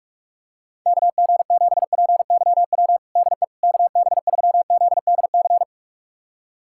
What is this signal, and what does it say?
Morse code, heard as a keyed tone, about 46 wpm, 690 Hz. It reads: KG7PYW DE KB3ZDC
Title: USA Callsign DE Callsign - 6 Characters - 45wpm